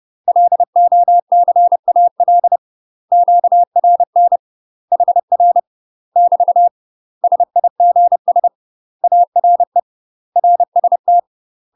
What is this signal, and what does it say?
Morse code, heard as a keyed tone, about 30 wpm, 700 Hz. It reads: LOCAL QRN HR <BT> SIGS ARE RST
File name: UK MASTER QSOs TRAINING / 30WPM / UK MASTER QSOs 0003